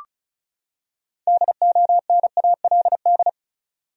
Morse code, heard as a keyed tone, about 34 wpm, 700 Hz. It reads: DONALD
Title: USA Common Names - 35wpm